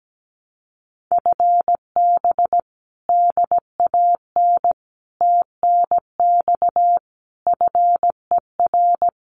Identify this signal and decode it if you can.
Morse code, heard as a keyed tone, about 17 wpm, 700 Hz. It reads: FB DAN TNX FER
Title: Common QSO Phrases Encore 17wpm